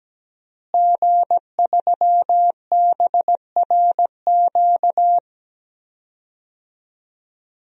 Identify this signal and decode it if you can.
Morse code, heard as a keyed tone, about 17 wpm, 700 Hz. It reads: G3BRQ
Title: DXCC-100 Call Signs - 5 Characters - Repeated 2x - 17wpm